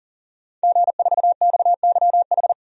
Morse code, heard as a keyed tone, about 40 wpm, 700 Hz. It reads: G4XYH
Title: DXCC-100 Call Signs - 5 Characters - 40wpm